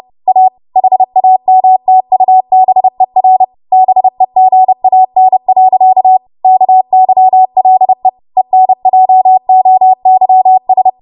Morse code, heard as a keyed tone, about 30 wpm, 750 Hz. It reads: A HAMTUBER BEGAN. KYLE ENJOYS
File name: Morse / 30WPM